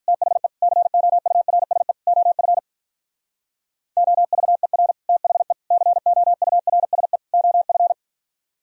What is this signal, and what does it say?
Morse code, heard as a keyed tone, about 44 wpm, 700 Hz. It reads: THE COURSE OF OVER THE COURSE OF